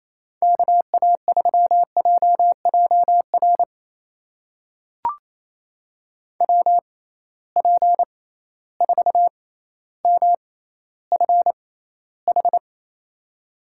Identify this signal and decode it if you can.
Morse code, heard as a keyed tone, about 28 wpm, spaced as for 14 wpm, 700 Hz. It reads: KA3JJR WP4MFH
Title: USA Call Signs - 6 Characters - 6x Character Spacing - 28wpm